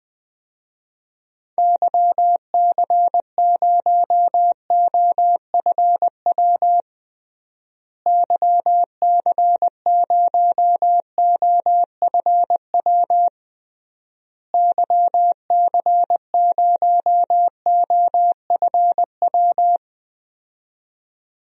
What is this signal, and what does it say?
Morse code, heard as a keyed tone, about 20 wpm, 700 Hz. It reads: YC0OFW YC0OFW YC0OFW
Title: DXCC-100 Call Signs - 6 Characters - Repeated 3x - 20wpm